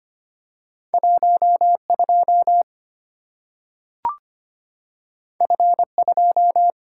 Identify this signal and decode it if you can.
Morse code, heard as a keyed tone, about 25 wpm, 700 Hz. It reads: 12 F2
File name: Two Letter-Number Combinations - 25wpm